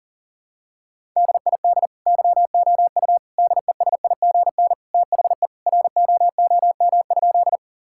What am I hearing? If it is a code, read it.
Morse code, heard as a keyed tone, about 37 wpm, 700 Hz. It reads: DID YOU DESIGN THE ROOM?
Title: Sentences from Top 1000 Words - Review Part 5 - 40wpm